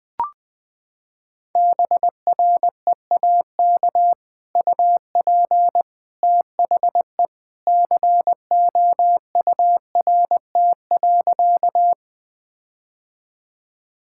Morse code, heard as a keyed tone, about 21 wpm, 700 Hz. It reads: BREAK UP THE COURT.
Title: Sentences from Top 1600 Words 20wpm